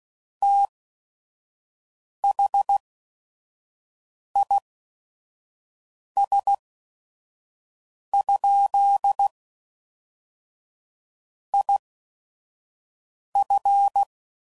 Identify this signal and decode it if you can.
Morse code, heard as a keyed tone, about 16 wpm, spaced as for 5 wpm, 800 Hz. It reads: THIS? IF
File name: CW Code Groups @5WPM / Technical Subjects / OR Gate